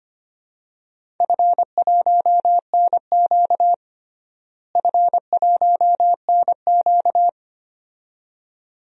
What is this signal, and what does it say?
Morse code, heard as a keyed tone, about 25 wpm, 700 Hz. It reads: F1NQ F1NQ
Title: DXCC-100 Call Signs - 4 Characters - Repeated 2x - 25wpm